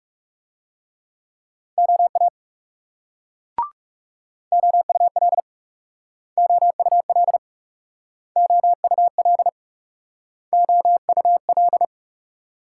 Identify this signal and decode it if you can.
Morse code, heard as a keyed tone, about 39 wpm, 700 Hz. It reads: OA OUL OUL OUL OUL